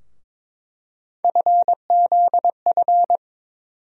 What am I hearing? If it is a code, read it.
Morse code, heard as a keyed tone, about 22 wpm, 700 Hz. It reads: FZF
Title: USA Call Sign Postfixes - 3 Letters 22wpm